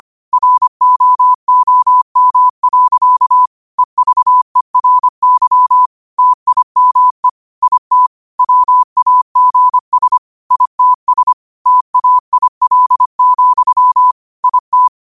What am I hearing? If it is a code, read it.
Morse code, heard as a keyed tone, about 25 wpm, 1000 Hz. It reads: ROOM. EVERY TIME IT WAGS ITS TAIL, IT
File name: qotdfiles / 25wpm